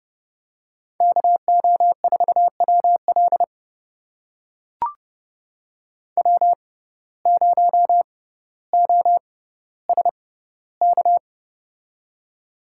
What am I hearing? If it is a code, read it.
Morse code, heard as a keyed tone, about 30 wpm, spaced as for 15 wpm, 700 Hz. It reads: KO4WL W0OSK
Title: USA Call Signs - 5 Characters - 6x Character Spacing - 30wpm